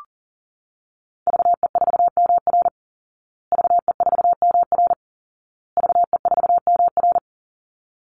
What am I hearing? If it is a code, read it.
Morse code, heard as a keyed tone, about 37 wpm, 700 Hz. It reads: VE4MR VE4MR VE4MR
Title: Call Signs Repeated 2x - 40wpm